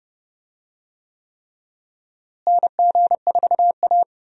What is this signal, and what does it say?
Morse code, heard as a keyed tone, about 32 wpm, 700 Hz. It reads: NG4A
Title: USA Call Signs - 4 Characters - Repeated 3x - 30wpm